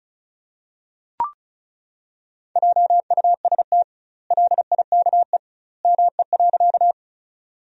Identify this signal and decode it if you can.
Morse code, heard as a keyed tone, about 36 wpm, 700 Hz. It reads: JUST LIKE ME.
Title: 2x Word-Spacing - Sentences from Top 200 Words - 35wpm